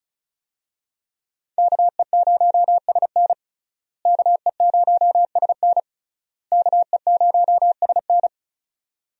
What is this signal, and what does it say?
Morse code, heard as a keyed tone, about 34 wpm, 700 Hz. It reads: KE0SN KE0SN KE0SN
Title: USA Call Signs - 5 Characters - Repeated 3x - 35wpm